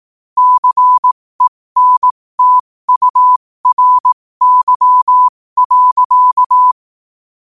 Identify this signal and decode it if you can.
Morse code, heard as a keyed tone, about 18 wpm, spaced as for 15 wpm, 1000 Hz. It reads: CENTURY.
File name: 15wpm